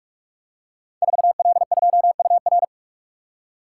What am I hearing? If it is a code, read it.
Morse code, heard as a keyed tone, about 44 wpm, 700 Hz. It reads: 4L2UR
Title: Code Groups - 5 Letter:Numbers - 45wpm